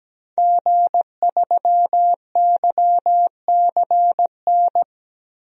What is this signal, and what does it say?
Morse code, heard as a keyed tone, about 17 wpm, 700 Hz. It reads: G3YCN
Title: DXCC-100 Call Signs - 5 Characters - 17wpm